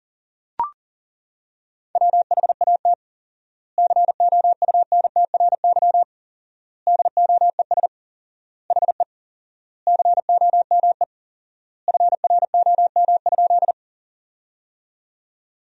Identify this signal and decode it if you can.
Morse code, heard as a keyed tone, about 40 wpm, 700 Hz. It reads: WHAT COUNTRY DOES HE COME FROM?
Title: 4x Word-Spacing - Sentences from Top 300 Words - 40wpm